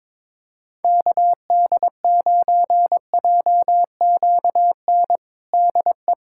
Morse code, heard as a keyed tone, about 22 wpm, 700 Hz. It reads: KD9JQN DE
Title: Practice QSO - Exchange 4 - Ending - 22wpm